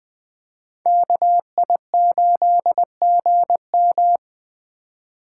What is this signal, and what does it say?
Morse code, heard as a keyed tone, about 25 wpm, 700 Hz. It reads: KI8GM